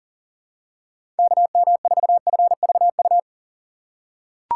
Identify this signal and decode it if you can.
Morse code, heard as a keyed tone, about 40 wpm, 700 Hz. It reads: KM4FVU
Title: USA Call Signs - 6x Character Spacing - 40wpm